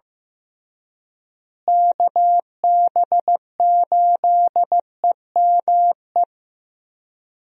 Morse code, heard as a keyed tone, about 14 wpm, 700 Hz. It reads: KB8EME
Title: USA Call Signs - 6 Characters - Repeated 3x - 15wpm